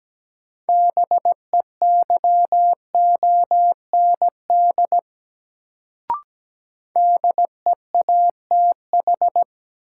Morse code, heard as a keyed tone, about 17 wpm, 700 Hz. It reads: BEYOND DEATH
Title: Sentences from Top 1100 Words - New Word Drill Alt - 17wpm